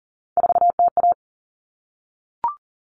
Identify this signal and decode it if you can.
Morse code, heard as a keyed tone, about 40 wpm, 700 Hz. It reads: 4TA
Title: All Cut Numbers - 3-Digit 40wpm